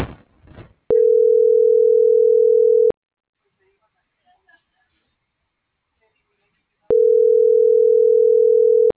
Ringback tone (North America), 440 + 480 Hz.